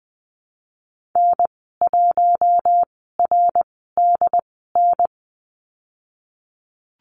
Morse code, heard as a keyed tone, about 20 wpm, spaced as for 15 wpm, 700 Hz. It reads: N1RDN
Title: 069 Lesson 12g - Letter d - Call Signs 20wpm